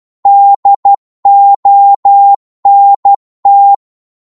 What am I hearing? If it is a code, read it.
Morse code, heard as a keyed tone, about 12 wpm, 800 Hz. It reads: DONT